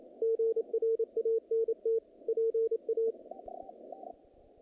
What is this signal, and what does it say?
Morse code, heard as a keyed tone, about 27 wpm, 450 Hz. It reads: GRANT PA
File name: contests / naqp-cw-k5tr-op-k5pi- -2100